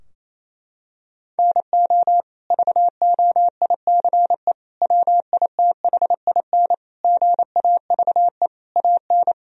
Morse code, heard as a keyed tone, about 28 wpm, 700 Hz. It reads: NO VOICE WITHIN GAVE AN
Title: Sentences from Top 1000 Words - Review Part 4 - 28wpm